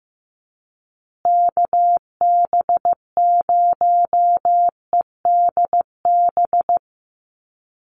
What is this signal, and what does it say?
Morse code, heard as a keyed tone, about 15 wpm, 700 Hz. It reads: KB0EDB
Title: USA Call Signs Repeated 3x 15wpm